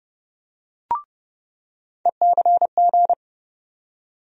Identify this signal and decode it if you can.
Morse code, heard as a keyed tone, about 32 wpm, 700 Hz. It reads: ECG
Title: USA Call Sign Postfixes - 3 Letters 30wpm